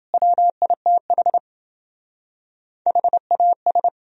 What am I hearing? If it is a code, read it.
Morse code, heard as a keyed tone, about 30 wpm, 700 Hz. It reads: WITH HAS